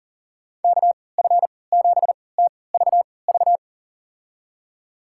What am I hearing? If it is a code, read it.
Morse code, heard as a keyed tone, about 40 wpm, spaced as for 23 wpm, 700 Hz. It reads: KF7TVV
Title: USA Call Signs - 3x Character Spacing - 40wpm